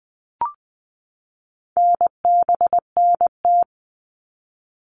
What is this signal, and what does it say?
Morse code, heard as a keyed tone, about 20 wpm, 700 Hz. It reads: NBNT
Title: All Cut Numbers - 4-Digit 20wpm